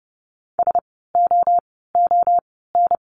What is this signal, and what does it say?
Morse code, heard as a keyed tone, about 30 wpm, 700 Hz. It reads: S O O N
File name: Lesson 16f - Letter c - Words 30wpm